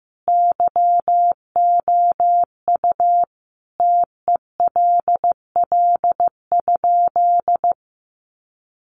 Morse code, heard as a keyed tone, about 15 wpm, 700 Hz. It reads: YOU TELL?
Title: Sentences from Top 300 Words 15wpm